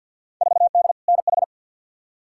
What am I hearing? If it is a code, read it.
Morse code, heard as a keyed tone, about 49 wpm, 700 Hz. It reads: 4D NH
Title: Field Day Exchanges Encore - 50wpm